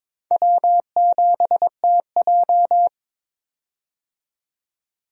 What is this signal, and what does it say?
Morse code, heard as a keyed tone, about 22 wpm, 700 Hz. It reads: W7TJ
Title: USA Call Signs - 4 Characters - 22wpm